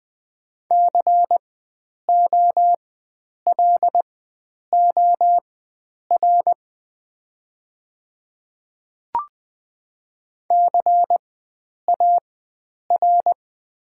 Morse code, heard as a keyed tone, about 20 wpm, spaced as for 9 wpm, 700 Hz. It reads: COLOR CAR
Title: 4x Character Spacing - Top 500 Words - 20wpm